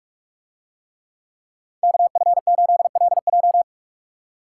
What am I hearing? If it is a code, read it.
Morse code, heard as a keyed tone, about 46 wpm, 700 Hz. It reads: KF8LJ